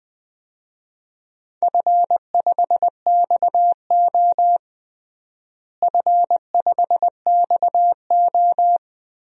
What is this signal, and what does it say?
Morse code, heard as a keyed tone, about 21 wpm, 700 Hz. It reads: F5XO F5XO
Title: DXCC-100 Call Signs - 4 Characters - Repeated 2x - 20wpm